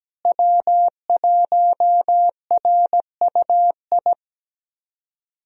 Morse code, heard as a keyed tone, about 17 wpm, 700 Hz. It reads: W1RUI
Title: USA Call Signs - 5 Characters - Repeated 2x - 17wpm